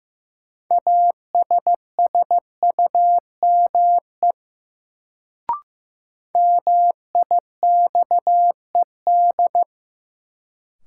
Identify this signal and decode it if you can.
Morse code, heard as a keyed tone, about 15 wpm, 700 Hz. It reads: ASSUME MIXED
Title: Sentences from Top 1500 Words - New Word Drill - 15wpm